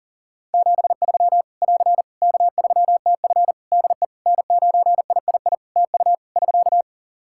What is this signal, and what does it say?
Morse code, heard as a keyed tone, about 40 wpm, 700 Hz. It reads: 73 <AR> K3TF DE N9III TU <SK>
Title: Practice QSO - Exchange 4 - Ending - 40wpm